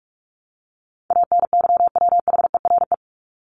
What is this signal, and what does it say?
Morse code, heard as a keyed tone, about 44 wpm, 700 Hz. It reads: ANYWHERE
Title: Top 1251-1500 Words 45wpm